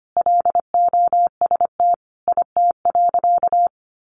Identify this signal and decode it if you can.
Morse code, heard as a keyed tone, about 25 wpm, 700 Hz. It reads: LOST IT.
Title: Sentences from Top 1400 Words 25wpm